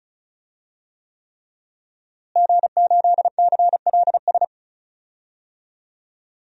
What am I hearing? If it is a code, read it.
Morse code, heard as a keyed tone, about 34 wpm, 700 Hz. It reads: G8CLS
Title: DXCC-100 Call Signs - 5 Characters - Repeated 3x - 35wpm